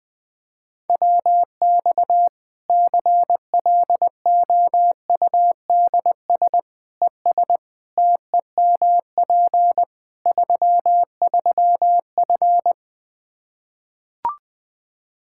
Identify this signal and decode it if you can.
Morse code, heard as a keyed tone, about 20 wpm, 700 Hz. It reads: WX CLOUDS ES TEMP 33F
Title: K1USN SST QSO Practice 20wpm